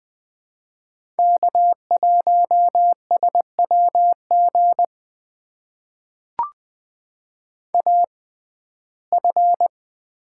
Morse code, heard as a keyed tone, about 20 wpm, spaced as for 14 wpm, 700 Hz. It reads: K1SWG AF
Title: USA Call Signs - 5 Characters - 6x Character Spacing - 20wpm